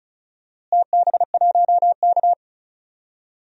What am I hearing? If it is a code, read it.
Morse code, heard as a keyed tone, about 34 wpm, 700 Hz. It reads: TB1K
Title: DXCC-100 Call Signs - 4 Characters - Repeated 2x - 35wpm